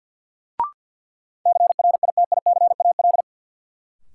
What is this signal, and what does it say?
Morse code, heard as a keyed tone, about 52 wpm, 700 Hz. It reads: CRITICAL
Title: Sentences from Top 1100 Words - New Word Drill Alt - 50wpm